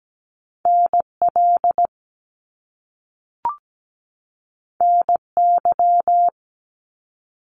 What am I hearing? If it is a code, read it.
Morse code, heard as a keyed tone, about 17 wpm, 700 Hz. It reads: NL NY
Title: Top Two Letter Combinations in Words 17wpm